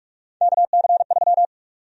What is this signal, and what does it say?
Morse code, heard as a keyed tone, about 47 wpm, 700 Hz. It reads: KC3
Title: USA Call Sign Prefixes - 45wpm